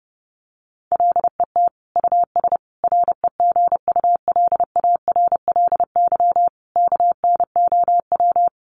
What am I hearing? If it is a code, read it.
Morse code, heard as a keyed tone, about 30 wpm, 700 Hz. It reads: LET US REGULARLY KNOW
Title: Sentences from Top 1400 Words 30wpm